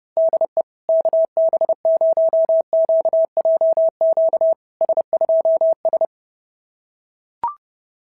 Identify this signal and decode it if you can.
Morse code, heard as a keyed tone, about 32 wpm, 650 Hz. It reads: DE KB0QJQ S2S
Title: SOTA QSO Practice 30wpm